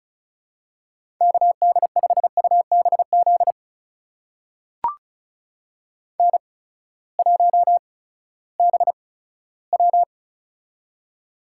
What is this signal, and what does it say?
Morse code, heard as a keyed tone, about 36 wpm, spaced as for 18 wpm, 700 Hz. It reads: KD5UBZ N1BW
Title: USA Call Signs - 8x Character Spacing - 35wpm